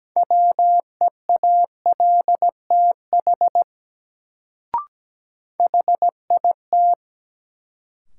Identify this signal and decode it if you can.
Morse code, heard as a keyed tone, about 17 wpm, 700 Hz. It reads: WEALTH HIT
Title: Sentences from Top 1200 Words - New Word Drill Alt - 17wpm